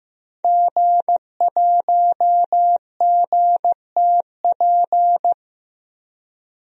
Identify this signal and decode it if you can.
Morse code, heard as a keyed tone, about 15 wpm, 700 Hz. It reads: G1GTP
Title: DXCC-100 Call Signs - 5 Characters - Repeated 3x - 15wpm